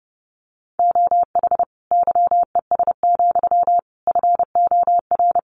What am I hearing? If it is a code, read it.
Morse code, heard as a keyed tone, about 30 wpm, 700 Hz. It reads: OH YES, FOR
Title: Sentences from Top 1300 Words 30wpm